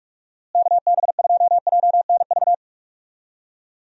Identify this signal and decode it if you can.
Morse code, heard as a keyed tone, about 45 wpm, 700 Hz. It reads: KB2JNV
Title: USA Call Signs - 6 Characters - Repeated 3x - 45wpm